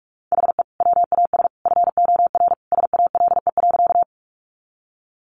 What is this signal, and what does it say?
Morse code, heard as a keyed tone, about 44 wpm, 700 Hz. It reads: HE WAS FOR SALE.
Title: Sentences from Top 1400 Words 45wpm